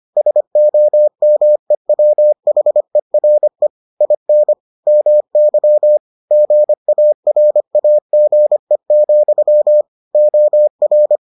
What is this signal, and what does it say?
Morse code, heard as a keyed tone, about 25 wpm, 600 Hz. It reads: SOMEWHERE IN MY GARAGE, OR